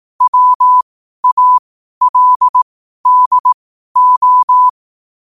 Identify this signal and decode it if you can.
Morse code, heard as a keyed tone, about 18 wpm, spaced as for 12 wpm, 1000 Hz. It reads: WALDO
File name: qotdfiles / 12wpm